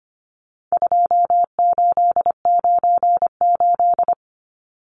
Morse code, heard as a keyed tone, about 25 wpm, 700 Hz. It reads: 2898
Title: Numbers - 4-Digit 25wpm